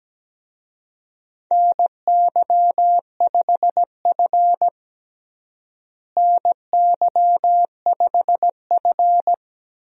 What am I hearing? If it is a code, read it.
Morse code, heard as a keyed tone, about 17 wpm, 700 Hz. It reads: NY5F NY5F
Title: USA Call Signs - 4 Characters - Repeated 3x - 17wpm